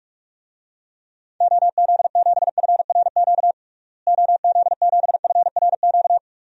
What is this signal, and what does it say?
Morse code, heard as a keyed tone, about 42 wpm, 700 Hz. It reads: OZ7FRQ OZ7FRQ
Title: DXCC-100 Call Signs - 6 Characters - Repeated 3x - 45wpm